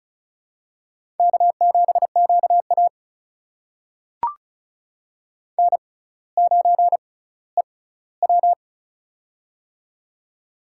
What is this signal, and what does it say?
Morse code, heard as a keyed tone, about 36 wpm, spaced as for 18 wpm, 700 Hz. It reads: K7QA N9EW
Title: USA Call Signs - 4 Characters - 6x Character Spacing - 35wpm